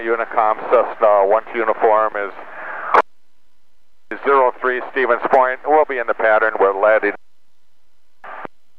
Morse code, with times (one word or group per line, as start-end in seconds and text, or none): none